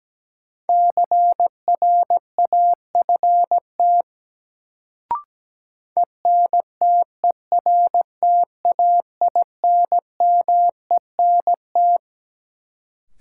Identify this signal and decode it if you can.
Morse code, heard as a keyed tone, about 17 wpm, 700 Hz. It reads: CRAFT ENTERTAINMENT
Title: Sentences from Top 1000 Words - Review Part 6 - New Word Drill - 17wpm